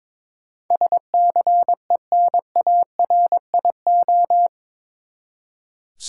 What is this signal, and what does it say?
Morse code, heard as a keyed tone, about 22 wpm, 700 Hz. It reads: SCENARIO
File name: practice / 22wpm / Eight Letter Words - 22wpm